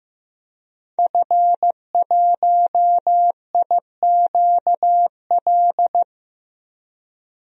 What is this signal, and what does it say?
Morse code, heard as a keyed tone, about 15 wpm, 700 Hz. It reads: F1IQL
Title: DXCC-100 Call Signs - 5 Characters - Repeated 2x - 15wpm